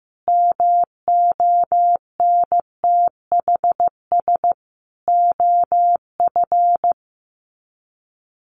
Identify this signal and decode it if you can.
Morse code, heard as a keyed tone, about 15 wpm, 700 Hz. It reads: MONTHS OF
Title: Sets of 5 Words Encore2 15wpm